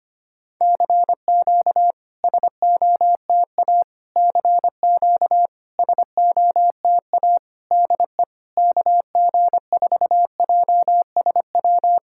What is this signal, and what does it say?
Morse code, heard as a keyed tone, about 25 wpm, 700 Hz. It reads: CQ SOTA CQ SOTA DE KG4JSW
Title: SOTA QSO Practice 25wpm